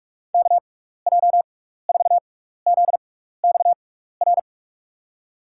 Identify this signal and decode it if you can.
Morse code, heard as a keyed tone, about 47 wpm, spaced as for 16 wpm, 690 Hz. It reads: KJ4ZXR
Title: USA Call Signs - 6x Character Spacing - 45wpm